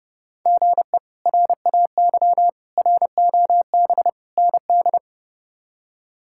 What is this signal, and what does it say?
Morse code, heard as a keyed tone, about 32 wpm, 700 Hz. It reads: GE RAY ROB ND
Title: K1USN SST QSO Practice - 30wpm